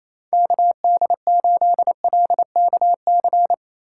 Morse code, heard as a keyed tone, about 28 wpm, 700 Hz. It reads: KD8LKC
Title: USA Call Signs - 6 Characters - 3x Character Spacing - 28wpm